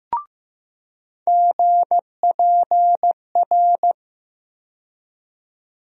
Morse code, heard as a keyed tone, about 15 wpm, 700 Hz. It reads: GPR
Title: USA Call Sign Postfixes - 3 Letters 15wpm